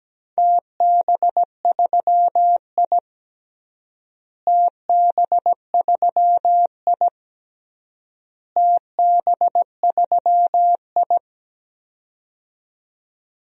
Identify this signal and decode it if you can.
Morse code, heard as a keyed tone, about 17 wpm, 700 Hz. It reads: TB3I TB3I TB3I
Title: DXCC-100 Call Signs - 4 Characters - Repeated 3x - 17wpm